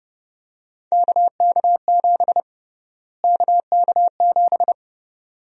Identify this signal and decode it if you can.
Morse code, heard as a keyed tone, about 31 wpm, 700 Hz. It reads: KK7 KK7
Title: USA Call Sign Prefixes - Repeated 2x - 30wpm